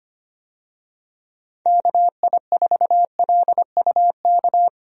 Morse code, heard as a keyed tone, about 25 wpm, 700 Hz. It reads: KI4LUK
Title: USA Call Signs - 6 Characters - 25wpm